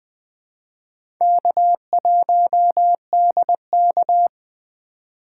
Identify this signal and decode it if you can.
Morse code, heard as a keyed tone, about 20 wpm, 700 Hz. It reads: K1DK
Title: USA Call Signs - 4 Characters - 4x Character Spacing - 20wpm